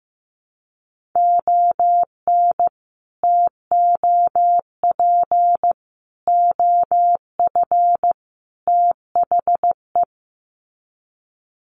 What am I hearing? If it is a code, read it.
Morse code, heard as a keyed tone, about 15 wpm, 700 Hz. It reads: ON TOP OF THE
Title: Sets of 4 Words 15wpm